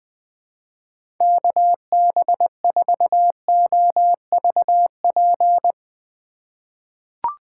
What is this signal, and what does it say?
Morse code, heard as a keyed tone, about 20 wpm, 700 Hz. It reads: KB4OVP
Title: USA Call Signs - 6 Characters - 2x Character Spacing - 20wpm